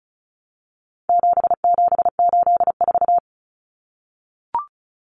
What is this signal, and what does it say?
Morse code, heard as a keyed tone, about 34 wpm, 700 Hz. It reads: 7784
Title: Common Cut Numbers - 4-Digit 35wpm